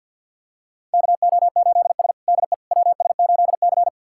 Morse code, heard as a keyed tone, about 46 wpm, 700 Hz. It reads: KO8S DE WS8C
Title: USA Callsign DE Callsign - 4 Characters - 50wpm